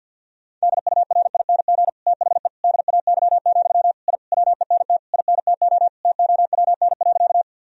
Morse code, heard as a keyed tone, about 53 wpm, 700 Hz. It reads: DURING THE DAY, I WENT INTO TOWN.
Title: Sentences from Top 1000 Words - Review Part 2 - 50wpm